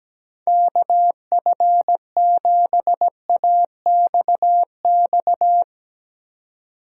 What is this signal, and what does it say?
Morse code, heard as a keyed tone, about 17 wpm, 700 Hz. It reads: KF7AXX
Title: USA Call Signs - 6 Characters - 2x Character Spacing - 17wpm